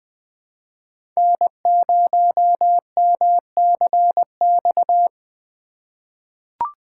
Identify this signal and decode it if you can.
Morse code, heard as a keyed tone, about 21 wpm, 700 Hz. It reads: N0MCX
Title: USA Call Signs - 5 Characters - 4x Character Spacing - 20wpm